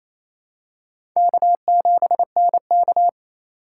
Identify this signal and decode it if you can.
Morse code, heard as a keyed tone, about 28 wpm, 700 Hz. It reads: K7NK
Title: USA Call Signs - 4 Characters - 4x Character Spacing - 28wpm